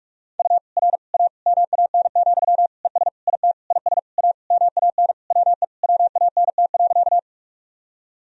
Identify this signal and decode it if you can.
Morse code, heard as a keyed tone, about 47 wpm, 700 Hz. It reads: U R A MAN, ES IT IS A MAN WE WANT.
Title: Sentences from Top 200 Words and Abbreviations 45wpm